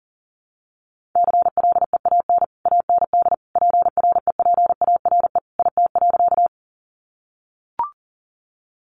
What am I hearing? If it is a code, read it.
Morse code, heard as a keyed tone, about 40 wpm, 700 Hz. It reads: CLEAN AND PREPARE IT.
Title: Sentences from Top 1100 Words 40wpm